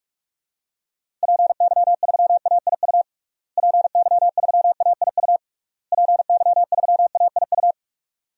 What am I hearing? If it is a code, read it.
Morse code, heard as a keyed tone, about 42 wpm, 700 Hz. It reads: PY3AIU PY3AIU PY3AIU
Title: DXCC-100 Call Signs - 6 Characters - Repeated 3x - 45wpm